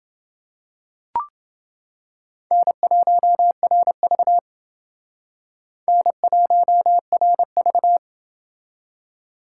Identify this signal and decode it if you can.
Morse code, heard as a keyed tone, about 29 wpm, 700 Hz. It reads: N1RV N1RV